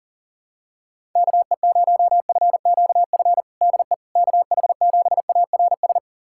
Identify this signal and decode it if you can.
Morse code, heard as a keyed tone, about 40 wpm, 700 Hz. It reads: KE0FQF DE KH7ARS
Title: USA Callsign DE Callsign - 6 Characters - 40wpm